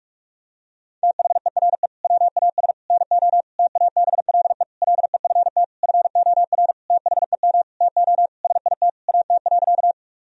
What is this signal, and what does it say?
Morse code, heard as a keyed tone, about 46 wpm, 700 Hz. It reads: THERE WAS NO TABLE LEFT FOR THEM TO SIT AT.
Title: Sentences from Top 1000 Words - Review Part 2 - 45wpm